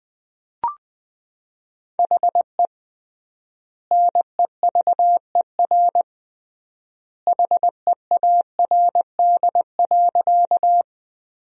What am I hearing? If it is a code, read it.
Morse code, heard as a keyed tone, about 20 wpm, 700 Hz. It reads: HE NEVER HEARD.
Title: 3x Word-Spacing - Sentences from Top 400 Words - 20wpm